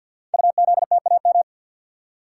Morse code, heard as a keyed tone, about 48 wpm, 700 Hz. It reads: UZTAM
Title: Code Groups - 5 Letters - 50wpm